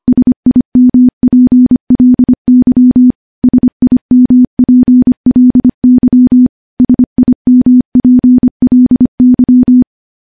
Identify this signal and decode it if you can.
Morse code, heard as a keyed tone, about 25 wpm, 250 Hz. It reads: SIMPLY SIMPLY SIMPLY
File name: morse / morse-training-25WPM